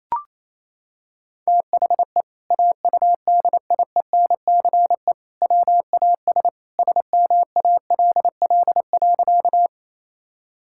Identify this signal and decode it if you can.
Morse code, heard as a keyed tone, about 28 wpm, 700 Hz. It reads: THE AUDIENCE WAS SMALL.
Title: Sentences from Top 1000 Words - Review Part 5 - 28wpm